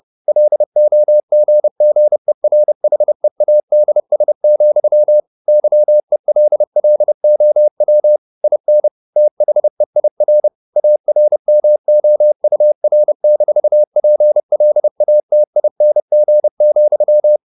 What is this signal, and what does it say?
Morse code, heard as a keyed tone, about 30 wpm, 600 Hz. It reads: LOGGERHEADS, YELLOW IN THEIR ARMOUR-PLATING,
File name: the old man and the sea / the old man and the sea 1 30wpm